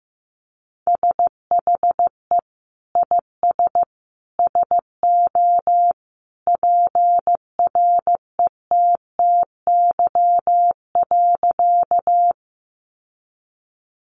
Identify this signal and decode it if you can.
Morse code, heard as a keyed tone, about 14 wpm, 700 Hz. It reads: SHE IS SO PRETTY.
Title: Sentences from Top 1000 Words 15wpm